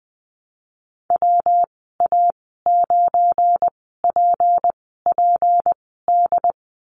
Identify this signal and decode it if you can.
Morse code, heard as a keyed tone, about 20 wpm, spaced as for 15 wpm, 700 Hz. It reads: WA9PPD
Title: 207 Lesson 33g - Letter j - Call Signs 20wpm